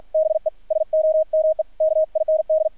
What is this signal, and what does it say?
Morse code, heard as a keyed tone, about 53 wpm, 630 Hz, with radio noise. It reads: BE SOME KIND
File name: archived / CRSnet